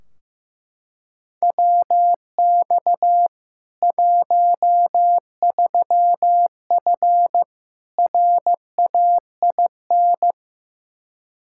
Morse code, heard as a keyed tone, about 15 wpm, 700 Hz. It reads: WX 13F RAIN
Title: K1USN SST QSO Practice 15wpm